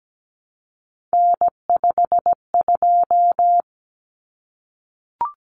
Morse code, heard as a keyed tone, about 17 wpm, 700 Hz. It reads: N52
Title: Common Cut Numbers - 3-Digit 17wpm